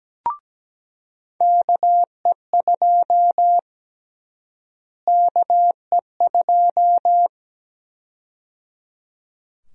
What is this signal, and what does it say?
Morse code, heard as a keyed tone, about 17 wpm, 700 Hz. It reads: KE2 KE2
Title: USA Call Sign Prefixes - Repeated 2x - 17wpm